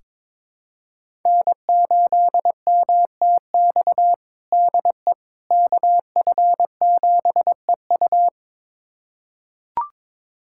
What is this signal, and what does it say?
Morse code, heard as a keyed tone, about 21 wpm, 700 Hz. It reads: N8MTX DE KF7EU
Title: USA Callsign DE Callsign - 5 Characters - 22wpm